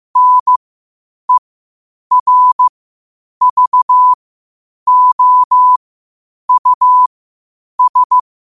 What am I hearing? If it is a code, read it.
Morse code, heard as a keyed tone, about 15 wpm, spaced as for 8 wpm, 1000 Hz. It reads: NERVOUS
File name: qotdfiles / 8wpm